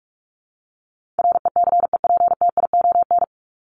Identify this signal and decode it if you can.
Morse code, heard as a keyed tone, about 44 wpm, 700 Hz. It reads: RECEPTION
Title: Top 1751-2000 Words 45wpm